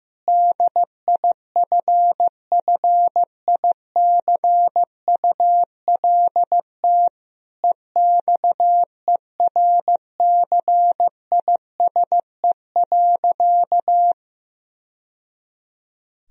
Morse code, heard as a keyed tone, about 15 wpm, 700 Hz. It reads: DIFFICULT EXERCISE.
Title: Sentences from Top 1000 Words - Review Part 4 - 15wpm